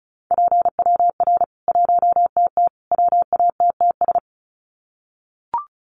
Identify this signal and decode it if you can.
Morse code, heard as a keyed tone, about 34 wpm, 700 Hz. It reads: PWR 1TT WATTS
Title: Common QSO Phrases 35wpm